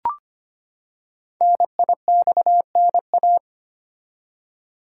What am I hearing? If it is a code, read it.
Morse code, heard as a keyed tone, about 25 wpm, 700 Hz. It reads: NIXNA
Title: Code Groups - 5 Letters - 25wpm